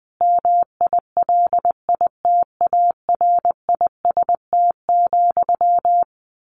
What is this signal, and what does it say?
Morse code, heard as a keyed tone, about 20 wpm, 700 Hz. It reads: MILITARIST,
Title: A Sound Of Thunder 20wpm